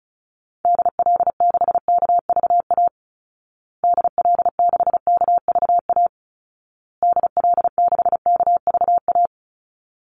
Morse code, heard as a keyed tone, about 34 wpm, 700 Hz. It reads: DL6KVA DL6KVA DL6KVA
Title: Call Signs Repeated 2x - 35wpm